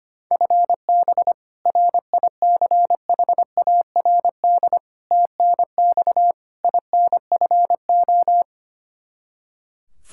Morse code, heard as a keyed tone, about 25 wpm, 700 Hz. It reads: FB RICHARD TNX INFO
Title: Practice QSO - Exchange 3 - More Info - 25wpm